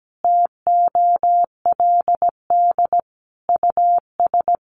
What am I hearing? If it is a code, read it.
Morse code, heard as a keyed tone, about 17 wpm, 700 Hz. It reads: TOLD US
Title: Sentences from Top 400 Words 17wpm